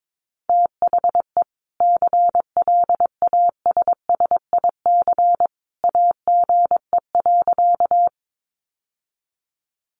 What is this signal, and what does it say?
Morse code, heard as a keyed tone, about 22 wpm, 700 Hz. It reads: THE CLASSIC AGE.
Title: Sentences from Top 1000 Words 22wpm